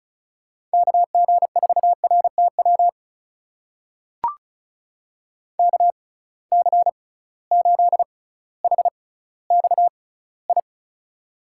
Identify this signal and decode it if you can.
Morse code, heard as a keyed tone, about 36 wpm, spaced as for 18 wpm, 700 Hz. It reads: KG4RTW KC8HXI
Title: USA Call Signs - 6 Characters - 6x Character Spacing - 35wpm